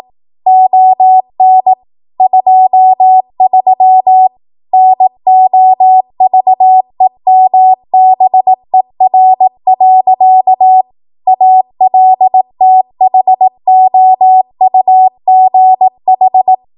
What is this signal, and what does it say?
Morse code, heard as a keyed tone, about 18 wpm, 750 Hz. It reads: ON 23 NOVEMBER. ALTHOUGH
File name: Archive / 18 WPM / 18WPM